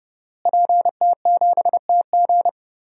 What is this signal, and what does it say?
Morse code, heard as a keyed tone, about 31 wpm, 700 Hz. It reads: PT7TG
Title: DXCC-100 Call Signs - 5 Characters - Repeated 2x - 30wpm